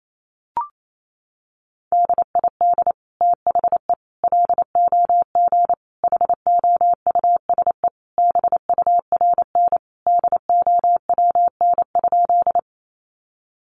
Morse code, heard as a keyed tone, about 28 wpm, 700 Hz. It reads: DID THE LOG HOUSE BURN DOWN?
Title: Sentences from Top 1400 Words 28wpm